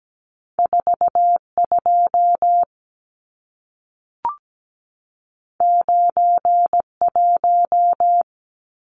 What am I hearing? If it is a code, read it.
Morse code, heard as a keyed tone, about 17 wpm, 700 Hz. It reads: 42 91
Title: Numbers - 2-Digit 17wpm